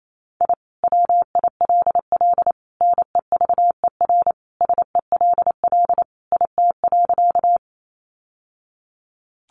Morse code, heard as a keyed tone, about 28 wpm, 700 Hz. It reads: I WILL NEVER SELL IT.
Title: Sentences from Top 1100 Words 28wpm